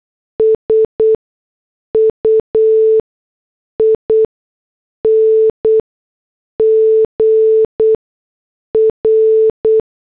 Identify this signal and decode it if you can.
Morse code, heard as a keyed tone, about 8 wpm, spaced as for 6 wpm, 440 Hz. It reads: SUING R